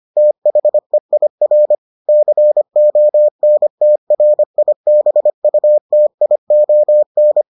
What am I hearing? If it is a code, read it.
Morse code, heard as a keyed tone, about 25 wpm, 600 Hz. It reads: THEIR CONTRIBUTION